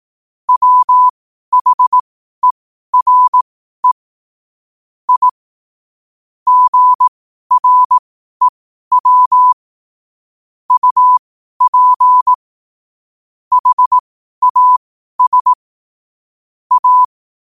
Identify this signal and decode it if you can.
Morse code, heard as a keyed tone, about 18 wpm, spaced as for 12 wpm, 1000 Hz. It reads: WHERE I GREW UP HAS A
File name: qotdfiles / 12wpm